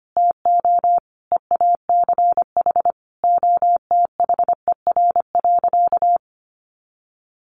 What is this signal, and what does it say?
Morse code, heard as a keyed tone, about 25 wpm, 700 Hz. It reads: TO EACH OTHER.
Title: Easy Phrases 25wpm